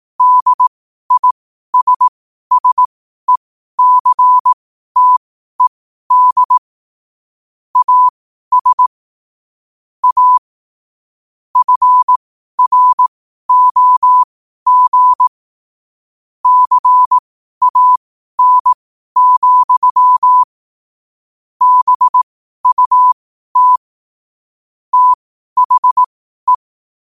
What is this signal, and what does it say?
Morse code, heard as a keyed tone, about 18 wpm, spaced as for 12 wpm, 1000 Hz. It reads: DISSECTED AS A FROG CAN, BUT THE
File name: qotdfiles / 12wpm